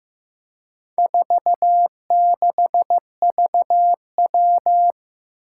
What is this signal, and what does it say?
Morse code, heard as a keyed tone, about 15 wpm, 700 Hz. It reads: 46VW
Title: Four Challenging Letter-Number Combinations 15wpm